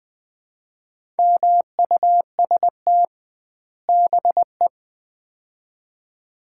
Morse code, heard as a keyed tone, about 20 wpm, 700 Hz. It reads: MUST BE
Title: 2x Word-Spacing - Sets of 2 Words - 20wpm